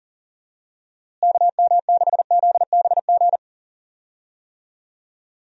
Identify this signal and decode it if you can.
Morse code, heard as a keyed tone, about 41 wpm, 700 Hz. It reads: KM6ZBG
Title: USA Call Signs - 6 Characters - Repeated 2x - 40wpm